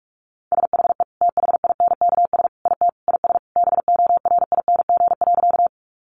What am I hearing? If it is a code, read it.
Morse code, heard as a keyed tone, about 46 wpm, 700 Hz. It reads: SHE THINKS IT IS BORING.
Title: Easy Phrases 45wpm